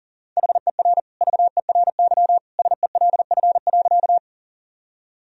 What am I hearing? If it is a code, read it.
Morse code, heard as a keyed tone, about 40 wpm, 700 Hz. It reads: HER VERY SELF.
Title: Sentences from Top 1000 Words - Review Part 1 - 40wpm